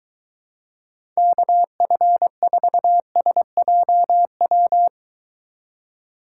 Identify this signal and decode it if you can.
Morse code, heard as a keyed tone, about 23 wpm, 700 Hz. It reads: KF4SJW